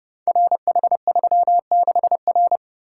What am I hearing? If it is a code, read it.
Morse code, heard as a keyed tone, about 30 wpm, 700 Hz. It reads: RH36R
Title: Code Groups - 5 Letter:Numbers - 30wpm